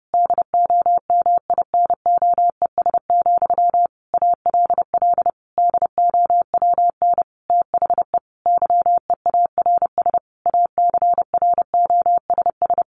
Morse code, heard as a keyed tone, about 30 wpm, 700 Hz. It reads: DOMINOES, ALL DOWN THE YEARS ACROSS
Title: A Sound Of Thunder 30wpm